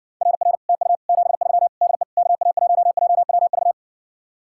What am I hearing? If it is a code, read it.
Morse code, heard as a keyed tone, about 60 wpm, 700 Hz. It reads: RR TU 73 DE KA1JWV